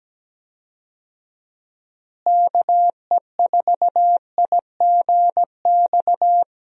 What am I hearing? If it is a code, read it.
Morse code, heard as a keyed tone, about 17 wpm, 700 Hz. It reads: KE4IGX
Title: USA Call Signs - 6 Characters - Repeated 3x - 17wpm